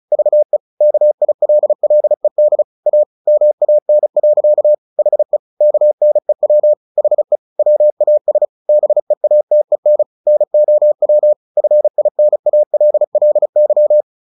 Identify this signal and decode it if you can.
Morse code, heard as a keyed tone, about 34 wpm, 600 Hz. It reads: VE KILLED A MAN. HE KNEW HE WAS BEATEN NOW FINALLY
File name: the old man and the sea 2 35wpm